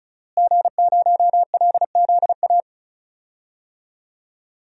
Morse code, heard as a keyed tone, about 36 wpm, 700 Hz. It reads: G0LZA
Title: DXCC-100 Call Signs - 5 Characters - Repeated 2x - 35wpm